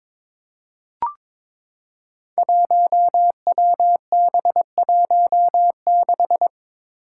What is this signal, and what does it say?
Morse code, heard as a keyed tone, about 22 wpm, 700 Hz. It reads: 1WB16
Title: Five Challenging Letter-Number Combinations 22wpm